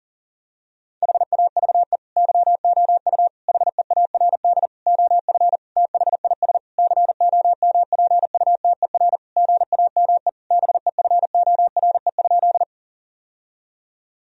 Morse code, heard as a keyed tone, about 37 wpm, 700 Hz. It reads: HAVE YOU HEARD OF THIS COMPUTER GAME BEFORE?
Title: Sentences from Top 1000 Words - Review Part 3 - 40wpm